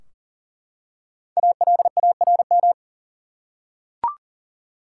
Morse code, heard as a keyed tone, about 40 wpm, 700 Hz. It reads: ALARM
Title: Five Letter Words - 40wpm